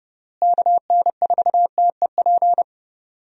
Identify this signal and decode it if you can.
Morse code, heard as a keyed tone, about 30 wpm, 700 Hz. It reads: KN4TEP
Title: USA Call Signs - 6 Characters - Repeated 3x - 30wpm